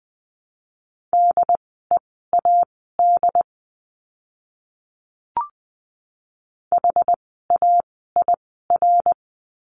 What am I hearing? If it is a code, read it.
Morse code, heard as a keyed tone, about 20 wpm, spaced as for 14 wpm, 700 Hz. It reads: DEAD HAIR
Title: 075 Lesson 13f - Letter L - Words 20wpm